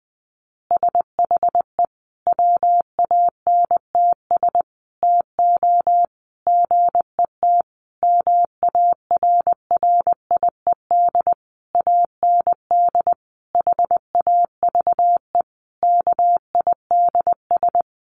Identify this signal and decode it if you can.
Morse code, heard as a keyed tone, about 20 wpm, 700 Hz. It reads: SHE WANTS TO GET MARRIED AND HAVE KIDS
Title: Easy Phrases 20wpm